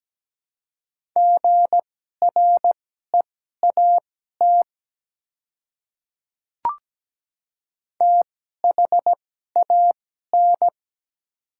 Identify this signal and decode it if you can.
Morse code, heard as a keyed tone, about 17 wpm, spaced as for 12 wpm, 700 Hz. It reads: GREAT THAN
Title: 2x Character Spacing - Top 300 Words - 17wpm